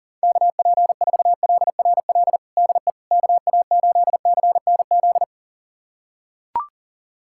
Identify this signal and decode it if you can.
Morse code, heard as a keyed tone, about 40 wpm, 700 Hz. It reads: KP4LRL DE KA8CNZ
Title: USA Callsign DE Callsign 40wpm